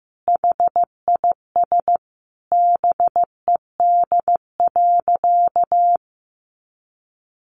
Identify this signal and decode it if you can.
Morse code, heard as a keyed tone, about 15 wpm, 700 Hz. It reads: HIS BED.
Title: Sentences from Top 1100 Words 15wpm